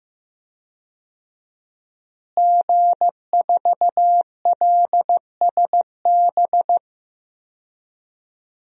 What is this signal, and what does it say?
Morse code, heard as a keyed tone, about 15 wpm, 700 Hz. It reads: G4LSB
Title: DXCC-100 Call Signs - 5 Characters - Repeated 3x - 15wpm